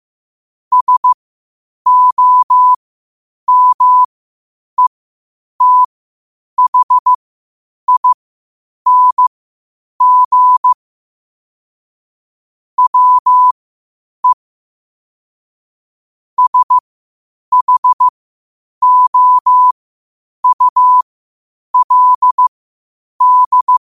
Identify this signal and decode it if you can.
Morse code, heard as a keyed tone, about 15 wpm, spaced as for 8 wpm, 1000 Hz. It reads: SOMETHING WE SHOULD
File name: qotdfiles / 8wpm